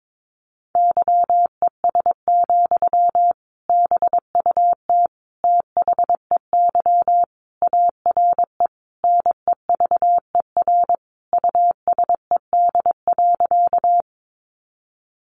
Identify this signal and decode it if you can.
Morse code, heard as a keyed tone, about 22 wpm, 700 Hz. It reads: YES, BUT THEY ARE NEVER USED.
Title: Sentences from Top 600 Words 22wpm